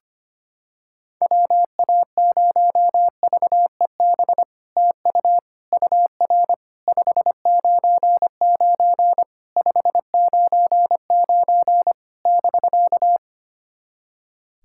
Morse code, heard as a keyed tone, about 25 wpm, 690 Hz. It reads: WA0VEB TU UR 599 599 <BK>
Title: SOTA QSO Practice 25wpm